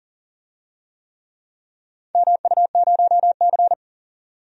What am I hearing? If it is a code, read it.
Morse code, heard as a keyed tone, about 37 wpm, 700 Hz. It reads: MU0C
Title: DXCC-100 Call Signs - 4 Characters - 40wpm